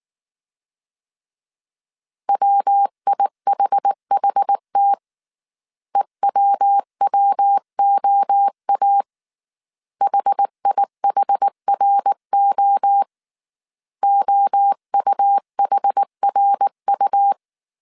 Morse code, heard as a keyed tone, about 19 wpm, 800 Hz. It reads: WIHHT EWWOA HIHRO OUHRU
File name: CW / DET / 1 3 1500C S B 18wpm 1916